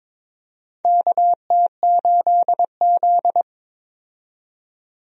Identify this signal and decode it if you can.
Morse code, heard as a keyed tone, about 22 wpm, 700 Hz. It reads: KT8Z
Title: USA Call Signs - 4 Characters - 22wpm